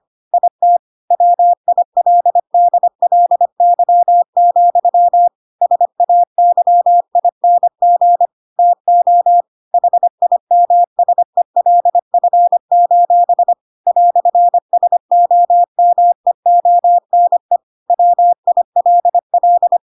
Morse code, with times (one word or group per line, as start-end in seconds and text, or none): 0.3-0.8 IT
1.1-5.3 WILDLY,
5.6-8.3 SAYING
8.6-9.4 TO
9.7-13.5 HIMSELF:
13.9-17.6 "SOMEONE
17.9-19.8 WILL